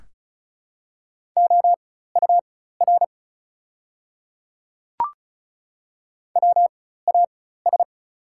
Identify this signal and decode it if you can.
Morse code, heard as a keyed tone, about 34 wpm, spaced as for 16 wpm, 690 Hz. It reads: OUR WAS
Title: 4x Character Spacing - Top 200 Words - 35wpm